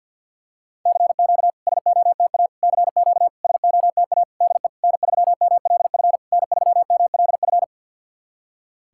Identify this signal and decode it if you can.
Morse code, heard as a keyed tone, about 48 wpm, 690 Hz. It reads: CQ SOTA CQ SOTA DE N3MLF N3MLF
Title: SOTA QSO Practice 50wpm